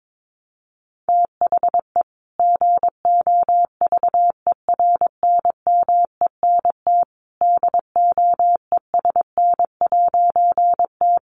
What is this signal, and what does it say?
Morse code, heard as a keyed tone, about 22 wpm, 700 Hz. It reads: THE GOVERNMENT DOESN'T
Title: A Sound Of Thunder 22wpm